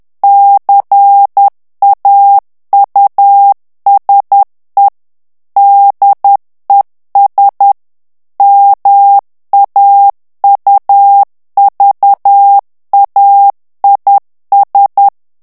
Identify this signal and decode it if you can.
Morse code, heard as a keyed tone, about 10 wpm, 800 Hz. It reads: CAUSE DES MAUVAIS
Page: Français - texte en clair 1 - vitesse : 10 mots minute